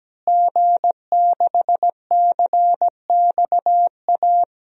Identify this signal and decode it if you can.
Morse code, heard as a keyed tone, about 17 wpm, 700 Hz. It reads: G6CXA
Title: DXCC-100 Call Signs - 5 Characters - Repeated 2x - 17wpm